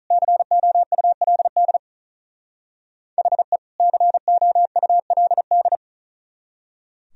Morse code, heard as a keyed tone, about 44 wpm, 700 Hz. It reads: COULD HE COULD
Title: Speed-Racing - Sets of 2 Words 35wpm